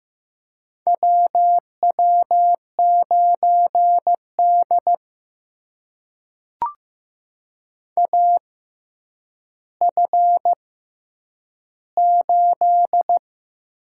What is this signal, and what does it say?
Morse code, heard as a keyed tone, about 15 wpm, spaced as for 9 wpm, 700 Hz. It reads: WW9D AF8
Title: USA Call Signs - 4 Characters - 6x Character Spacing - 15wpm